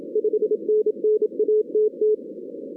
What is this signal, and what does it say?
Morse code, heard as a keyed tone, about 25 wpm, 440 Hz. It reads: 5NN ATT
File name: 2016 arrl dx cw / 2016-ARRL-DX-CW-K5TR- - 0-41